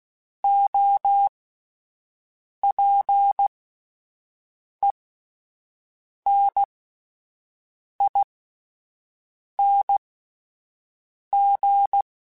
Morse code, heard as a keyed tone, about 16 wpm, spaced as for 5 wpm, 800 Hz. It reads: OPENING
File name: Alphabetical / Alphabetical mp3s / Os @5